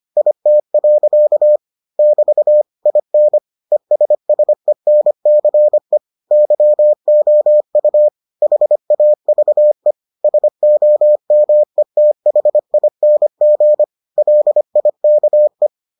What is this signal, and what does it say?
Morse code, heard as a keyed tone, about 25 wpm, 600 Hz. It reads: IT. <BT> IN ESSENCE YOU HAVE SOMETHING LIKE